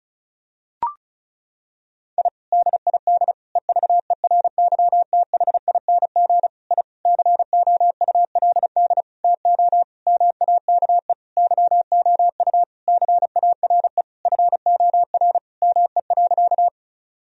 Morse code, heard as a keyed tone, about 36 wpm, 700 Hz. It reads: I DID EVERYTHING I COULD TO MAKE YOU CARE FOR ME.
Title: Sentences from Top 1000 Words - Review Part 1 - 35wpm